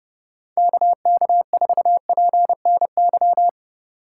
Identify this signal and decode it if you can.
Morse code, heard as a keyed tone, about 30 wpm, 700 Hz. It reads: KK4PNY
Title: USA Call Signs - 6 Characters - 6x Character Spacing - 30wpm